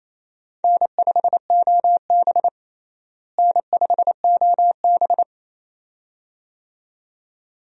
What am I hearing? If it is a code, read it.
Morse code, heard as a keyed tone, about 28 wpm, 700 Hz. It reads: N5OB N5OB
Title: USA Call Signs - 4 Characters - Repeated 2x - 28wpm